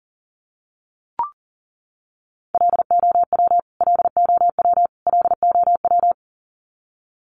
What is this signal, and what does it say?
Morse code, heard as a keyed tone, about 40 wpm, 700 Hz. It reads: LOW LOW LOW
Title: Top 500 Words Repeated 40wpm